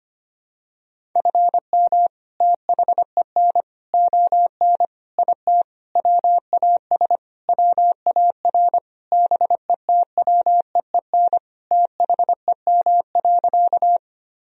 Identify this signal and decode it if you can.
Morse code, heard as a keyed tone, about 25 wpm, 700 Hz. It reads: FM THEN ON IT WAS WAR BETWEEN THEM.
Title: Sentences from Top 400 Words and Abbreviations 25wpm